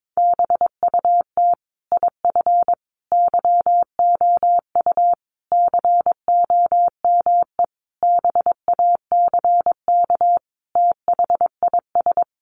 Morse code, heard as a keyed tone, about 22 wpm, 700 Hz. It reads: BUT IF YOU COME BACK THIS
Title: A Sound Of Thunder 22wpm